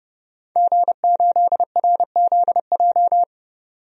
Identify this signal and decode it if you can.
Morse code, heard as a keyed tone, about 32 wpm, 700 Hz. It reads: G8RZJ
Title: DXCC-100 Call Signs - 5 Characters - 30wpm